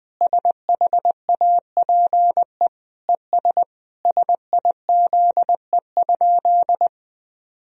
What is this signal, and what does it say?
Morse code, heard as a keyed tone, about 19 wpm, 700 Hz. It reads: SHAPE ES SIZE?
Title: Sentences from Top 500 Words and Abbreviations 20wpm